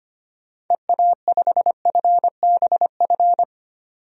Morse code, heard as a keyed tone, about 25 wpm, 700 Hz. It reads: EA5FBF
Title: DXCC-100 Call Signs - 6 Characters - 25wpm